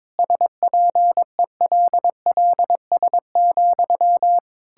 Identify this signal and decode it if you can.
Morse code, heard as a keyed tone, about 21 wpm, 700 Hz. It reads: SPELLS,
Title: Pangram Challenge v2 - 22wpm